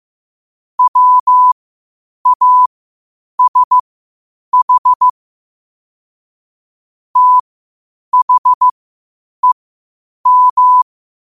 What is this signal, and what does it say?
Morse code, heard as a keyed tone, about 15 wpm, spaced as for 8 wpm, 1000 Hz. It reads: WASH THEM
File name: qotdfiles / 8wpm